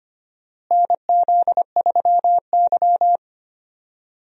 Morse code, heard as a keyed tone, about 25 wpm, 700 Hz. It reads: NZ3Y
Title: USA Call Signs - 4 Characters - Repeated 2x - 25wpm